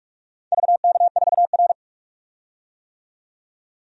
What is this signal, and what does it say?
Morse code, heard as a keyed tone, about 46 wpm, 700 Hz. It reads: VK4R
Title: DXCC-100 Call Signs - 4 Characters - 45wpm